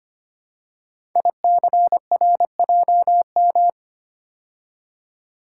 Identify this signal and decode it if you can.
Morse code, heard as a keyed tone, about 25 wpm, 700 Hz. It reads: ICRJM
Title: Code Groups - 5 Letters - 25wpm